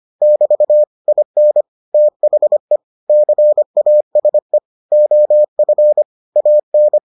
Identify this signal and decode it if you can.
Morse code, heard as a keyed tone, about 25 wpm, 600 Hz. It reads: <BT> IN THE CASE OF AN